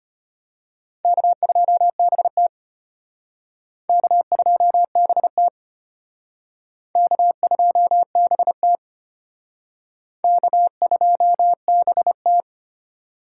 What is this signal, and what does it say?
Morse code, heard as a keyed tone, about 32 wpm, 700 Hz. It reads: K2BT K2BT K2BT K2BT